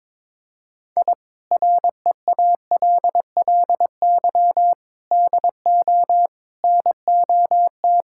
Morse code, heard as a keyed tone, about 21 wpm, 700 Hz. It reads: I REALLY DO NOT
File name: practice / 22wpm / Sentences with 4-Word Phrases - 22wpm